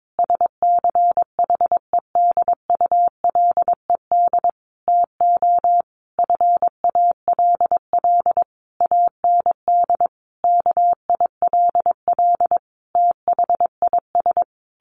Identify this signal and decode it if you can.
Morse code, heard as a keyed tone, about 22 wpm, 700 Hz. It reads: SCHEDULED TO FALL AND KILL THIS
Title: A Sound Of Thunder 22wpm